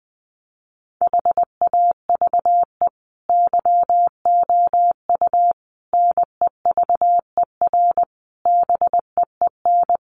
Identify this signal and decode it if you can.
Morse code, heard as a keyed tone, about 20 wpm, 700 Hz. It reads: HAVE YOU NEVER BEEN
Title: Sentences from Top 1400 Words 20wpm